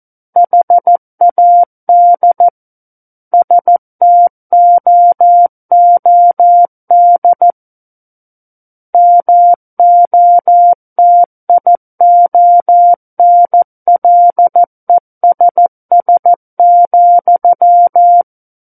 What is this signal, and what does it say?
Morse code, heard as a keyed tone, about 14 wpm, 700 Hz. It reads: HAD STOOD MOTIONLESS,
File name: Book One / 05-Chapter 5 - 14 WPM